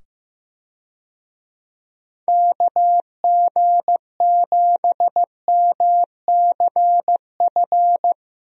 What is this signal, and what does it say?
Morse code, heard as a keyed tone, about 15 wpm, 700 Hz. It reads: KG7MCF
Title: USA Call Signs - 6 Characters - 15wpm